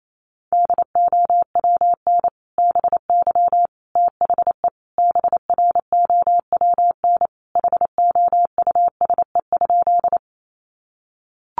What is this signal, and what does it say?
Morse code, heard as a keyed tone, about 28 wpm, 700 Hz. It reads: DOWN BY THE BROWN HOUSE?
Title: Sentences from Top 1200 Words 28wpm